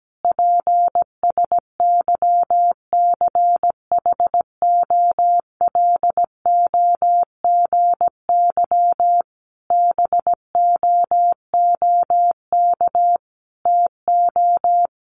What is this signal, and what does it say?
Morse code, heard as a keyed tone, about 17 wpm, 700 Hz. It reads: PSYCHOLOGY BOOK TO
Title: Sentences from Top 1400 Words 17wpm